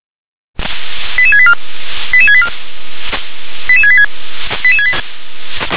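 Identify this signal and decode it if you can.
Five-tone selective call (ZVEI), heard as quick runs of five tones, tones of about 70 ms. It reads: 88684 88684 88686
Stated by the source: ZVEI five-tone selective call 88770 received on an SDR.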